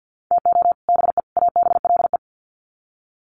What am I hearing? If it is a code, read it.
Morse code, heard as a keyed tone, about 50 wpm, 700 Hz. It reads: TO BE ABLE
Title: Sets of 3 Words 50wpm